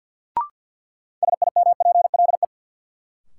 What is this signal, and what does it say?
Morse code, heard as a keyed tone, about 53 wpm, 700 Hz. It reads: SIMPLE
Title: Sentences from Top 400 Words - New Word Drill - 50wpm